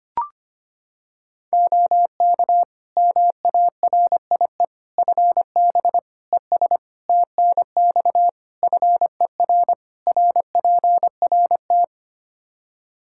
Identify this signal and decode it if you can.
Morse code, heard as a keyed tone, about 25 wpm, 700 Hz. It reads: OK MARIE FB ES TNX FER RPRT
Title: Practice QSO - Exchange 2 - Info - 25wpm